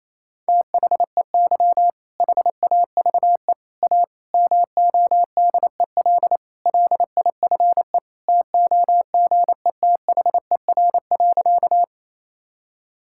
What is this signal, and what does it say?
Morse code, heard as a keyed tone, about 28 wpm, 700 Hz. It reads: THEY HAVE A MODEL LIFE TOGETHER.
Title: Sentences from Top 1000 Words - Review Part 5 - 28wpm